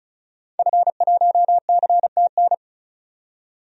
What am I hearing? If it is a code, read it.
Morse code, heard as a keyed tone, about 34 wpm, 700 Hz. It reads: F1CTN
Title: DXCC-100 Call Signs - 5 Characters - 35wpm